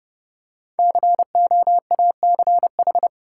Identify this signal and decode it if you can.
Morse code, heard as a keyed tone, about 31 wpm, 700 Hz. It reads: COACH
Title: Five Letter Words - 30wpm